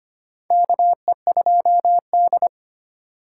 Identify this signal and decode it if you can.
Morse code, heard as a keyed tone, about 25 wpm, 700 Hz. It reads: KE2D
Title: K1USN SST QSO Practice - 25wpm